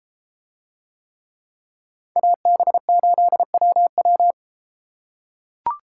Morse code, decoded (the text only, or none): AB8WW